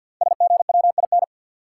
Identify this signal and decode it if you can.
Morse code, heard as a keyed tone, about 53 wpm, 700 Hz. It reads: SGWIN
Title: Code Groups - 5 Letters - 50wpm